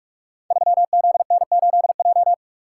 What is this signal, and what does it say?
Morse code, heard as a keyed tone, about 44 wpm, 700 Hz. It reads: 3ZN8J
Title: Code Groups - 5 Letter:Numbers - 45wpm